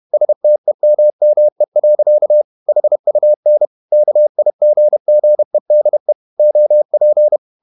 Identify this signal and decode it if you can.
Morse code, heard as a keyed tone, about 31 wpm, 600 Hz. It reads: STEMME. HUN KIGGEDE OP
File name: Ferie QRQ 31wpm